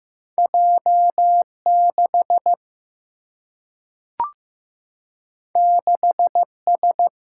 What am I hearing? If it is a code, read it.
Morse code, heard as a keyed tone, about 15 wpm, 700 Hz. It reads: J 6 6 S
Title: Single-Letter Single-Number 15wpm